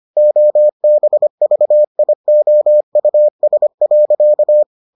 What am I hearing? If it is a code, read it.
Morse code, heard as a keyed tone, about 25 wpm, 600 Hz. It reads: OBVIOUS.